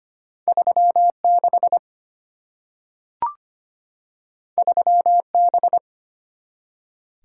Morse code, heard as a keyed tone, about 25 wpm, 700 Hz. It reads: 36 3B
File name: Two Letter-Number Combinations - 25wpm